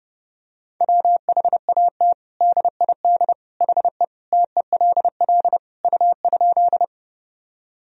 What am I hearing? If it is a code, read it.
Morse code, heard as a keyed tone, about 30 wpm, 700 Hz. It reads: WHAT DID HE TELL U?
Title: Sentences from Top 200 Words and Abbreviations 30wpm